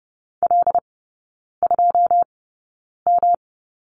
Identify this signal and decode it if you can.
Morse code, heard as a keyed tone, about 30 wpm, spaced as for 9 wpm, 700 Hz. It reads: L2M
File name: Lesson 18e - Letter m - Three Characters 30wpm